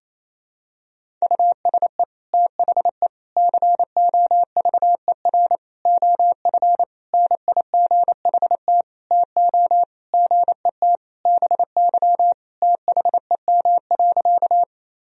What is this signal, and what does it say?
Morse code, heard as a keyed tone, about 28 wpm, 700 Hz. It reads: USE THE COVER OF NIGHT TO GET BY THEM.
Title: Sentences from Top 1000 Words - Review Part 1 - 28wpm